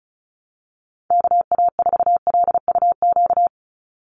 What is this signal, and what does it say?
Morse code, heard as a keyed tone, about 34 wpm, 700 Hz. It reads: KA4LUQ
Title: USA Call Signs 35wpm